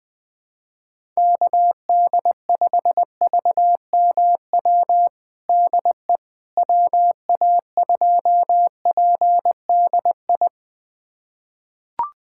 Morse code, heard as a keyed tone, about 20 wpm, 700 Hz. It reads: KD5VMW DE WA2PDI
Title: USA Callsign DE Callsign - 6 Characters - 20wpm